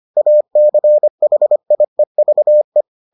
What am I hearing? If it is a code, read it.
Morse code, heard as a keyed tone, about 25 wpm, 600 Hz. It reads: ACHIEVE